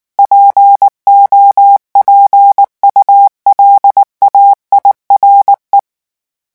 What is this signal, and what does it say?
Morse code, heard as a keyed tone, about 19 wpm, 800 Hz. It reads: POPULAIRE
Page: REVISION 8 - 8 mots minute